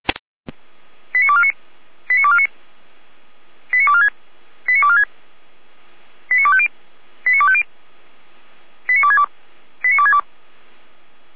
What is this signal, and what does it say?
Five-tone selective call (ZVEI), heard as quick runs of five tones, tones of about 70 ms. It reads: 89249 89249 89246 89246 89250 89250 89262 89262